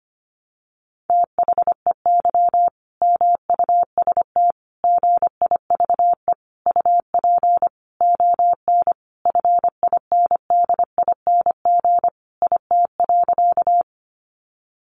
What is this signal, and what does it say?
Morse code, heard as a keyed tone, about 25 wpm, 700 Hz. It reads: THEY MUST GIVE UP ON FINDING IT.
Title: Sentences from Top 1100 Words 25wpm